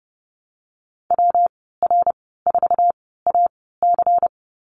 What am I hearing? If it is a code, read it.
Morse code, heard as a keyed tone, about 30 wpm, spaced as for 17 wpm, 700 Hz. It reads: WR4AC
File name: Lesson 19g - Letter w - Call Signs 30wpm